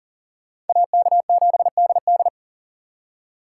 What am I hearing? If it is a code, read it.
Morse code, heard as a keyed tone, about 51 wpm, 700 Hz. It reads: AK7DD